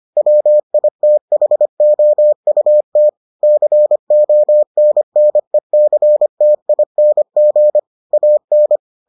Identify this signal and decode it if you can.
Morse code, heard as a keyed tone, about 25 wpm, 600 Hz. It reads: WITHOUT CONNECTING AN